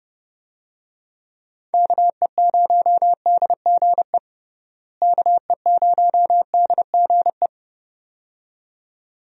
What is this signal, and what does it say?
Morse code, heard as a keyed tone, about 30 wpm, 700 Hz. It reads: KE0DGE KE0DGE
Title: USA Call Signs - 6 Characters - Repeated 2x - 30wpm